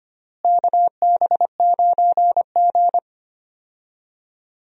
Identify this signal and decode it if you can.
Morse code, heard as a keyed tone, about 25 wpm, 700 Hz. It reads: KB9G
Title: USA Call Signs - 4 Characters - Repeated 2x - 25wpm